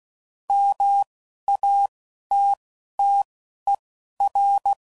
Morse code, heard as a keyed tone, about 16 wpm, spaced as for 11 wpm, 800 Hz. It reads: MATTER
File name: CW Code Groups @10WPM / Technical Subjects / AND Gate